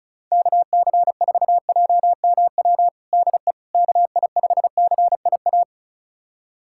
Morse code, heard as a keyed tone, about 36 wpm, 700 Hz. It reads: KC4JMW DE KI5CIA
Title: USA Callsign DE Callsign 35wpm